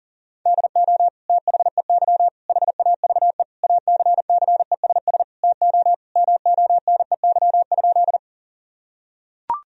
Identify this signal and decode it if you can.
Morse code, heard as a keyed tone, about 40 wpm, 690 Hz. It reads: DO THEY HAVE ACCESS TO MONEY?
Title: Sentences from Top 1000 Words - Review Part 4 - 40wpm